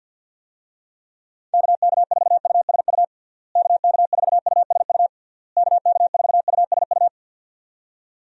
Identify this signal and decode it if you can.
Morse code, heard as a keyed tone, about 45 wpm, 700 Hz. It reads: KK4USU KK4USU KK4USU
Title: USA Call Signs - 6 Characters - Repeated 3x - 50wpm